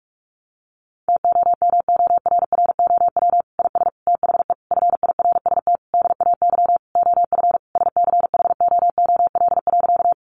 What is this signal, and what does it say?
Morse code, heard as a keyed tone, about 44 wpm, 700 Hz. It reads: TOMORROW IS THE FIRST DAY OF SCHOOL.
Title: Additional Easy Phrases 45wpm